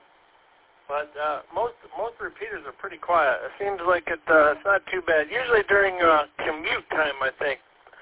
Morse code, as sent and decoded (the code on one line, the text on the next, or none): none